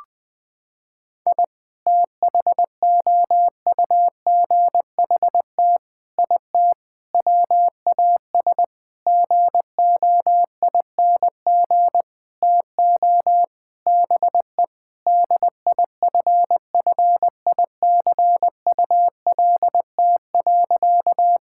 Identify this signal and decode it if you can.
Morse code, heard as a keyed tone, about 20 wpm, 700 Hz. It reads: I THOUGHT IT WAS GOING TO BE DIFFICULT.
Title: Sentences with 7-Word Phrases - 20wpm